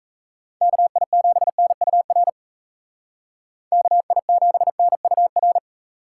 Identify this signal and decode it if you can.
Morse code, heard as a keyed tone, about 34 wpm, 700 Hz. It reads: KI7NUR KI7NUR
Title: Speed Racing - USA 2x3 Call Signs 28wpm